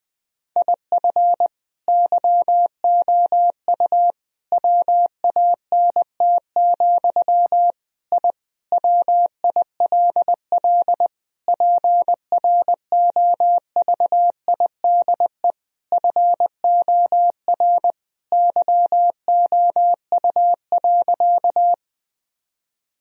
Morse code, heard as a keyed tone, about 21 wpm, 700 Hz. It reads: IF YOU WANT, I WILL PROVIDE FOR YOU.
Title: Sentences from Top 1000 Words - Review Part 3 - 20wpm